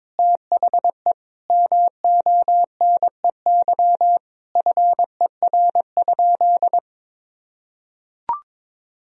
Morse code, heard as a keyed tone, about 21 wpm, 700 Hz. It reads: THE MONEY FER?
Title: Sentences from Top 400 Words and Abbreviations 22wpm